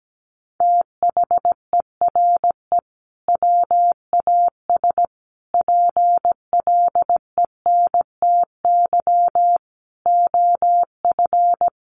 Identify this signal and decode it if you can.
Morse code, heard as a keyed tone, about 17 wpm, 700 Hz. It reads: THERE WAS PLENTY OF
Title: Sentences from Top 1300 Words 17wpm